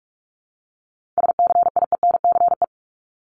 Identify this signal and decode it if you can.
Morse code, heard as a keyed tone, about 46 wpm, 700 Hz. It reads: SCIENCE
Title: Multi-letter Phonemes 45wpm